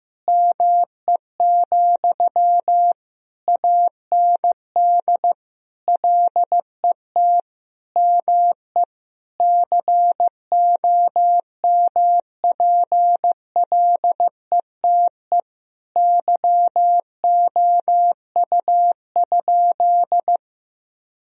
Morse code, heard as a keyed tone, about 15 wpm, 700 Hz. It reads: ME, AND LET ME COMPLETE YOU?
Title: Sentences from Top 1000 Words - Review Part 2 - 15wpm